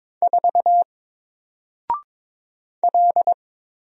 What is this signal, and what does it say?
Morse code, heard as a keyed tone, about 22 wpm, 700 Hz. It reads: 4 L
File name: Single-Letter-Number-22wpm